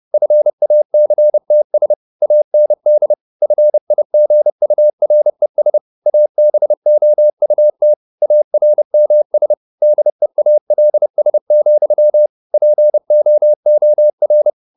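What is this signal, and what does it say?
Morse code, heard as a keyed tone, about 30 wpm, 600 Hz. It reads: FACTS AND FIGURES ABOUT ARMS DEALS, POOR
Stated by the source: Morse code at 30 wpm, the novel Never by Ken Follett in English, from CW med Gnister.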